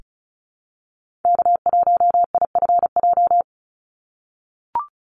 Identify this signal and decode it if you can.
Morse code, heard as a keyed tone, about 34 wpm, 700 Hz. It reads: K1IFJ
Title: Call Signs - 35wpm